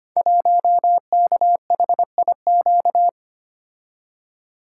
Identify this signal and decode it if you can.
Morse code, heard as a keyed tone, about 25 wpm, 700 Hz. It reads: 1KHIQ
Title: Code Groups - 5 Letter:Numbers - 25wpm